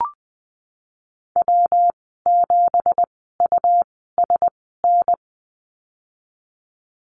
Morse code, heard as a keyed tone, about 20 wpm, spaced as for 15 wpm, 700 Hz. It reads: W7USN
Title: 179 Lesson 29g - Slash - Call Signs 20wpm